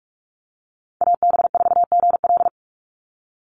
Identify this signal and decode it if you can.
Morse code, heard as a keyed tone, about 44 wpm, 700 Hz. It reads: AB4GL
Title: USA Call Signs 45wpm